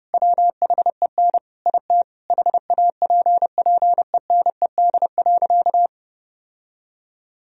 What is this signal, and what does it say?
Morse code, heard as a keyed tone, about 30 wpm, 700 Hz. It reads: WHEN IT HAPPENED.
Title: Sentences from Top 1000 Words - Review Part 4 - 30wpm